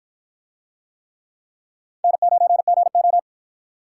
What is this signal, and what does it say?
Morse code, heard as a keyed tone, about 51 wpm, 700 Hz. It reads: N9GO